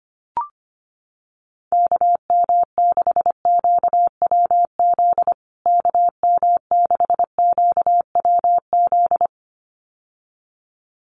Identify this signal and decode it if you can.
Morse code, heard as a keyed tone, about 25 wpm, 700 Hz. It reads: KM6QWZ KM6QWZ
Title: USA Call Signs Repeated 2x 25wpm